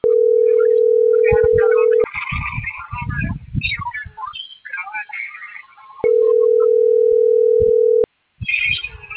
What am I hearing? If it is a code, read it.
Ringback tone (North America), 440 + 480 Hz.